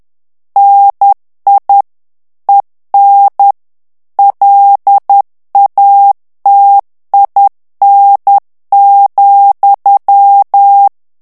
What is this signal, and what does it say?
Morse code, heard as a keyed tone, about 10 wpm, 800 Hz. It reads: NI EN LATIN,
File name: morse mp3 / FRANCAIS TEXTE CLAIR 2 VITESSE 10 MOTS MINUTES